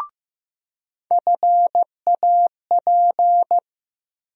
Morse code, heard as a keyed tone, about 15 wpm, 700 Hz. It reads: FAP
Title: USA Call Sign Postfixes - 3 Letters 15wpm